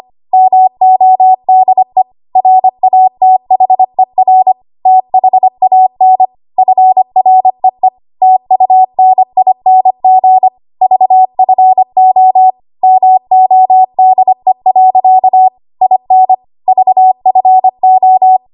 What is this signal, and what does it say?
Morse code, heard as a keyed tone, about 25 wpm, 750 Hz. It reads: MODE RATHER THAN FREE TUNING VFO MODE. IN VFO
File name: Archive / 25 WPM / 25WPM